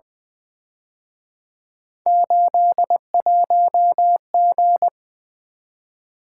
Morse code, heard as a keyed tone, about 23 wpm, 700 Hz. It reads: 81G
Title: Speed-Racing - Three Letter-Number Combinations - 20wpm